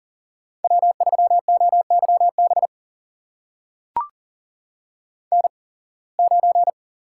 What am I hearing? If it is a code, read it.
Morse code, heard as a keyed tone, about 38 wpm, spaced as for 26 wpm, 700 Hz. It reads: W3OYB N9
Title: USA Call Signs - 5 Characters - 8x Character Spacing - 40wpm